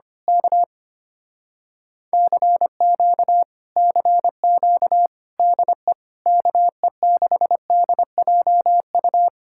Morse code, heard as a keyed tone, about 24 wpm, 700 Hz. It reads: K CQ CQ DE KE6DJU
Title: Speed-Racing Common QSO Phrases Encore - 25wpm